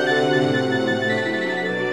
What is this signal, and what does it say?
There is no code.